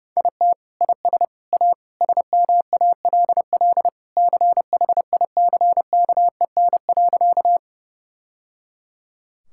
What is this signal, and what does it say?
Morse code, heard as a keyed tone, about 32 wpm, 700 Hz. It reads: IT IS A SMALL CHICKEN.
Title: Sentences from Top 1000 Words - Review Part 5 - 30wpm